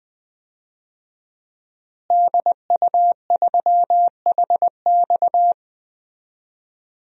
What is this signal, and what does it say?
Morse code, heard as a keyed tone, about 20 wpm, 700 Hz. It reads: DU3HX
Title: DXCC-100 Call Signs - 5 Characters - Repeated 2x - 20wpm